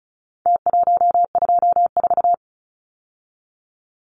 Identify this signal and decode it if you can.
Morse code, heard as a keyed tone, about 34 wpm, 700 Hz. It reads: T124
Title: Numbers - Contest Style - 4-Digit 35wpm